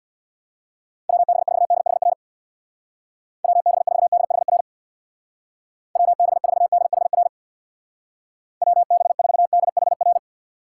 Morse code, heard as a keyed tone, about 62 wpm, 700 Hz. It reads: WB4DHR WB4DHR WB4DHR WB4DHR
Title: Speed Racing - USA 2x3 Call Signs 50wpm